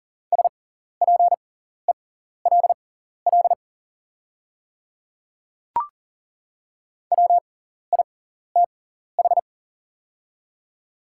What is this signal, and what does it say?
Morse code, heard as a keyed tone, about 39 wpm, spaced as for 14 wpm, 690 Hz. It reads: SPELL WITH
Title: 6x Character Spacing - Top 200 Words - 40wpm